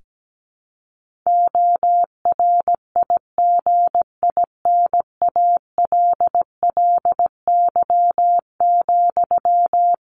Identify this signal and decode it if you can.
Morse code, heard as a keyed tone, about 17 wpm, 700 Hz. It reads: ORIGINALLY,
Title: Sentences from Top 1500 Words 17wpm